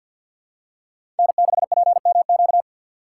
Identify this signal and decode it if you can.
Morse code, heard as a keyed tone, about 52 wpm, 700 Hz. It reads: N6PMQ
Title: USA Call Signs - 5 Characters - 4x Character Spacing - 50wpm